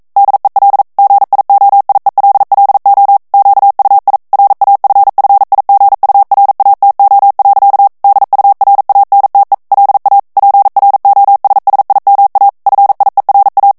Morse code, heard as a keyed tone, about 42 wpm, 800 Hz. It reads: DEL GIOIELLO QUI RAFFIGURATO. DURANTE LA PROSSIMA FIERA
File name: ITALIEN texte clair 1 vitesse 40 mots minute